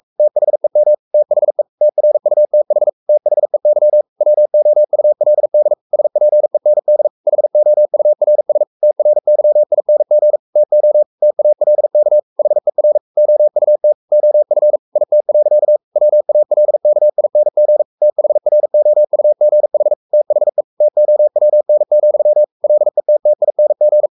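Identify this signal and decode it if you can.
Morse code, heard as a keyed tone, about 44 wpm, 600 Hz. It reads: THEM THE TRUTH THEY WOULD SPEND HOURS TRYING TO TALK HER OUT OF IT. WALKING THROUGH THE TOWN, LETTING
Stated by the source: Morse code at 43 wpm sending Ken Follett's novel Never in English, from CW med Gnister.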